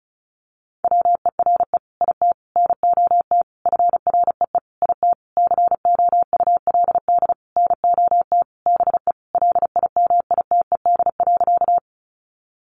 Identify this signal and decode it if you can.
Morse code, heard as a keyed tone, about 34 wpm, 700 Hz. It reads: WERE IT NOT FREE IT COULD NOT BE LIMITED.
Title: Sentences from Top 900 Words 35wpm